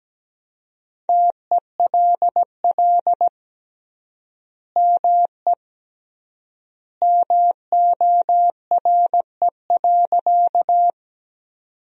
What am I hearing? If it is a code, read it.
Morse code, heard as a keyed tone, about 17 wpm, 700 Hz. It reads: TELL ME MORE.
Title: 3x Word-Spacing - Sentences from Top 300 Words - 17wpm